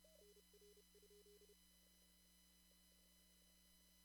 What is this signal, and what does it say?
Morse code, heard as a keyed tone, about 30 wpm, 410 Hz. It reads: NR?
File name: CW / kt5j wpx cw-op-k5pi